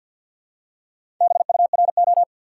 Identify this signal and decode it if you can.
Morse code, heard as a keyed tone, about 50 wpm, 700 Hz. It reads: BURO
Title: Top 90 QSO Elements 50wpm